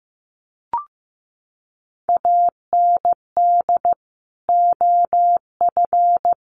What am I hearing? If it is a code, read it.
Morse code, heard as a keyed tone, about 15 wpm, 700 Hz. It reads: AND OF
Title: Sentences from Top 1500 Words 15wpm